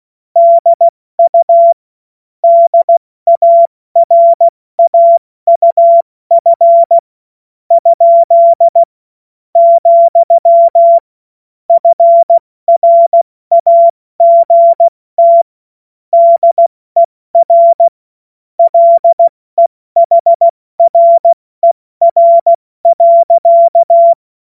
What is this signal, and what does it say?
Morse code, heard as a keyed tone, about 16 wpm, 680 Hz. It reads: DU DARAUF ? , FRAGT DER LEHRER.
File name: Morsekurs / Witze / Witz 13. September 2022 70 F80